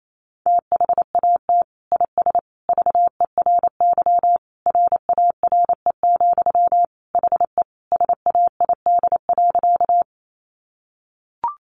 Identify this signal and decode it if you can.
Morse code, heard as a keyed tone, about 28 wpm, 700 Hz. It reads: THAT IS VERY RARE, HE SAID.
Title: Sentences from Top 1100 Words 28wpm